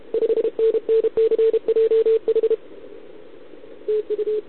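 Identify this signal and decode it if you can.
Morse code, heard as a keyed tone, about 32 wpm, 440 Hz. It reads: 5NN CJH TU
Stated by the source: Contest CW from OK1RI on 80 m during the OKOM DX contest, recorded by a QRP station using a KX3 and a wire antenna.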